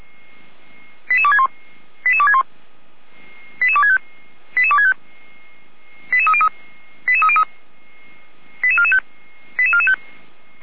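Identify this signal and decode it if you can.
Five-tone selective call (ZVEI), heard as quick runs of five tones, tones of about 70 ms. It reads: 88261 88261 88265 88265 88303 88303 88445 88445
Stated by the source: ZVEI five-tone selective call 88247 received on an SDR, from the Feuerwehr Orlamünde fire brigade.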